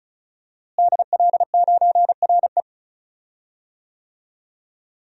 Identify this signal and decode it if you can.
Morse code, heard as a keyed tone, about 34 wpm, 700 Hz. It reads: DL9RE
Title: DXCC-100 Call Signs - 5 Characters - Repeated 2x - 35wpm